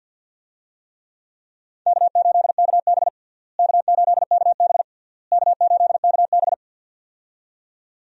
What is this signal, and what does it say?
Morse code, heard as a keyed tone, about 46 wpm, 700 Hz. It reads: K8KB K8KB K8KB
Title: USA Call Signs - 4 Characters - Repeated 3x - 50wpm